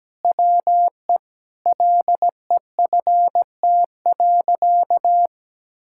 Morse code, heard as a keyed tone, about 17 wpm, 700 Hz. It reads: WE LEFT.
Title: Sentences from Top 1000 Words - Review Part 6 - 17wpm